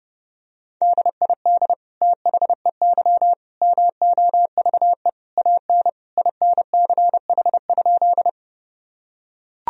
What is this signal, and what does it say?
Morse code, heard as a keyed tone, about 30 wpm, 700 Hz. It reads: DID THEY MOVE AN INCH?
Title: Sentences from Top 1000 Words - Review Part 3 - 30wpm